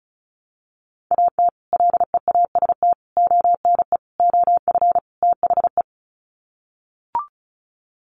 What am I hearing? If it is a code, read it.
Morse code, heard as a keyed tone, about 34 wpm, 700 Hz. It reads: AT LEAST ONE OF THE
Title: Sets of 5 Words Encore 35wpm